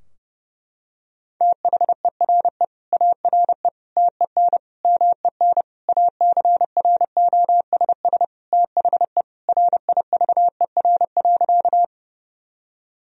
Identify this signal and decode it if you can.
Morse code, heard as a keyed tone, about 32 wpm, 700 Hz. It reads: THERE ARE TEN MEN ACROSS THE RIVER.
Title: Sentences from Top 1000 Words - Review Part 2 - 30wpm